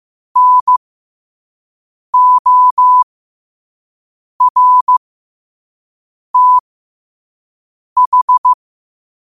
Morse code, heard as a keyed tone, about 15 wpm, spaced as for 5 wpm, 1000 Hz. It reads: NORTH
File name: qotdfiles / 5wpm